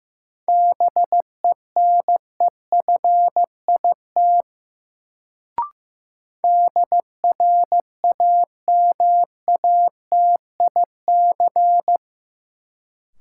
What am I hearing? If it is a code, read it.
Morse code, heard as a keyed tone, about 15 wpm, 700 Hz. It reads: BENEFIT DRAMATIC
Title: Sentences from Top 1300 Words - New Word Drill Alt - 15wpm